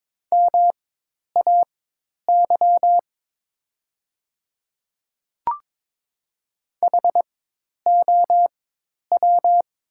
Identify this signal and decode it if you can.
Morse code, heard as a keyed tone, about 23 wpm, spaced as for 10 wpm, 700 Hz. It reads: MAY HOW
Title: 4x Character Spacing - Top 100 Words - 22wpm